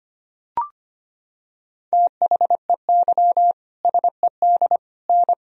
Morse code, heard as a keyed tone, about 25 wpm, 700 Hz. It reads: THEY SED N
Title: Sentences from Top 100 Words and Abbreviations 25wpm